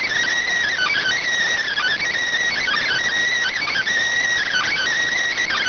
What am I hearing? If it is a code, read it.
Emergency Alert System (SAME) tones, heard as harsh buzzing bursts. It is heard from 0.7 to 4.8 s.